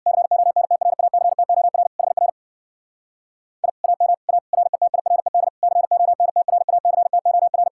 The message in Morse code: CONTRACTOR HR I AM A RETIRED CONTRACTOR

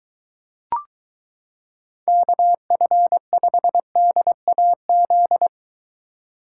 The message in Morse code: KF5DAZ

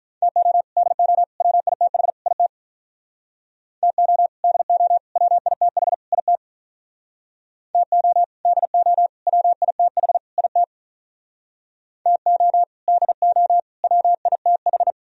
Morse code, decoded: TO DO WITH IT TO DO WITH IT TO DO WITH IT TO DO WITH

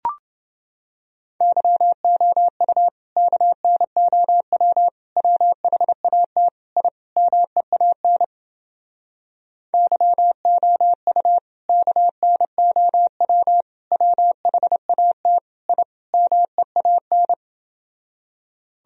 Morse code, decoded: YOU KNOW WHAT I MEAN YOU KNOW WHAT I MEAN

